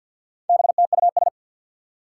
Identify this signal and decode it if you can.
Morse code, heard as a keyed tone, about 52 wpm, 700 Hz. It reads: BTUS
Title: Four Challenging Letter-Number Combinations 50wpm